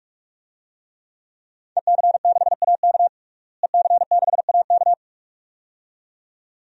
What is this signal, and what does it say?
Morse code, heard as a keyed tone, about 46 wpm, 700 Hz. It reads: EC6AK EC6AK
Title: DXCC-100 Call Signs - 5 Characters - Repeated 2x - 45wpm